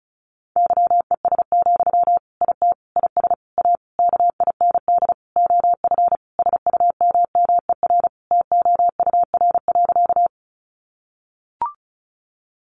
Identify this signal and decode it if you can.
Morse code, heard as a keyed tone, about 34 wpm, 700 Hz. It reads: YES, IT IS A KIND OF SUMMER TOUR.
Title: Sentences from Top 1400 Words 35wpm